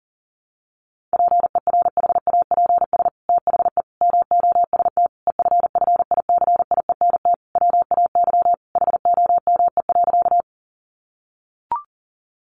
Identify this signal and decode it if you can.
Morse code, heard as a keyed tone, about 40 wpm, 700 Hz. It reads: PERHAPS THE MOST EFFICIENT WAY HOME.
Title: Sentences from Top 1300 Words 40wpm